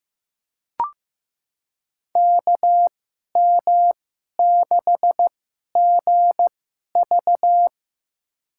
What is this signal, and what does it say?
Morse code, heard as a keyed tone, about 15 wpm, spaced as for 11 wpm, 700 Hz. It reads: KM6GV
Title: USA Call Signs - 2x Character Spacing - 15wpm